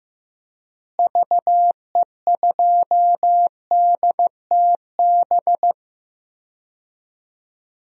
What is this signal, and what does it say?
Morse code, heard as a keyed tone, about 15 wpm, 700 Hz. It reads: VE2DTB
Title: DXCC-100 Call Signs - 6 Characters - Repeated 3x - 15wpm